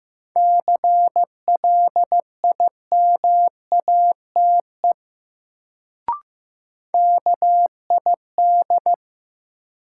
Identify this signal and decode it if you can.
Morse code, heard as a keyed tone, about 15 wpm, 700 Hz. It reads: CLIMATE KID
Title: Sentences from Top 1600 Words - New Word Drill - 15wpm